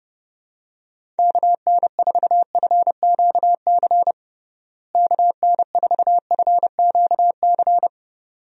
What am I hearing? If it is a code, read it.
Morse code, heard as a keyed tone, about 30 wpm, 700 Hz. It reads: KN4FQC KN4FQC
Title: USA Call Signs - 6 Characters - Repeated 2x - 30wpm